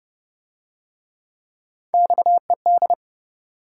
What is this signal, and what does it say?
Morse code, heard as a keyed tone, about 28 wpm, 700 Hz. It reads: XED